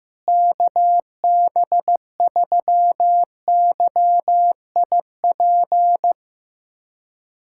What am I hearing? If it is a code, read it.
Morse code, heard as a keyed tone, about 15 wpm, 700 Hz. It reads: KB3YIP
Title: USA Callsign DE Callsign - 6 Characters - 15wpm